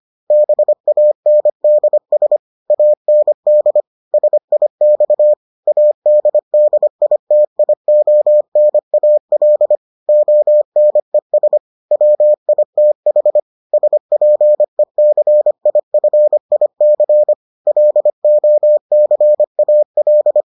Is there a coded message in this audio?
Yes: Morse code from 0.3 to 20.4 s, heard as a keyed tone, about 25 wpm, 600 Hz. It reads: BANDS AND SIX ADDITIONAL ONES WITH SPECIFIC LOCAL